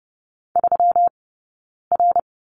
Morse code, heard as a keyed tone, about 30 wpm, 700 Hz. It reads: 3 R
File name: Lesson 20d - Number 3 - Two Characters 30wpm